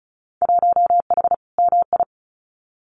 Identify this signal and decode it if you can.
Morse code, heard as a keyed tone, about 34 wpm, 700 Hz. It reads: 1H MI
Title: Winter Field Day Exchanges 35wpm